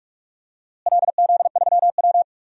Morse code, heard as a keyed tone, about 46 wpm, 700 Hz. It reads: LZ3W
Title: DXCC-100 Call Signs - 4 Characters - Repeated 3x - 45wpm